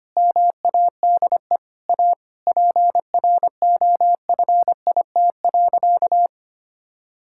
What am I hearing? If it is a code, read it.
Morse code, heard as a keyed tone, about 25 wpm, 700 Hz. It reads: MADE A PROFIT.
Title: Sentences from Top 1000 Words - Review Part 5 - 25wpm